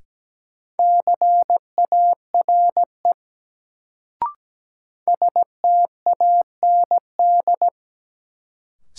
Morse code, heard as a keyed tone, about 17 wpm, 700 Hz. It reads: CARE STAND
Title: Sentences from Top 300 Words - New Word Drill - 17wpm